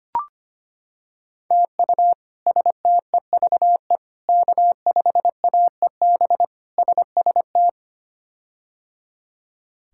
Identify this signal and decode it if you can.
Morse code, heard as a keyed tone, about 25 wpm, 700 Hz. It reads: TU STEVE K5AEB SST
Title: K1USN SST QSO Practice - 25wpm